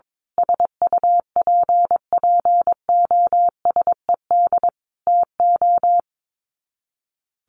Morse code, heard as a keyed tone, about 22 wpm, 700 Hz. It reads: SUPPOSED TO
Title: Sets of 5 Words Encore2 22wpm